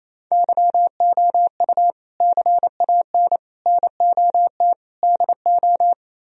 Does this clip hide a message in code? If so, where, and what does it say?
Yes: Morse code from 0.3 to 5.9 s, heard as a keyed tone, about 36 wpm, 700 Hz. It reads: YOU CAN NOT DO